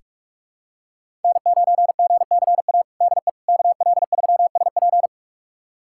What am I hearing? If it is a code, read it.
Morse code, heard as a keyed tone, about 43 wpm, 700 Hz. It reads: N9GCA DE KL3SP
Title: USA Callsign DE Callsign - 5 Characters - 45wpm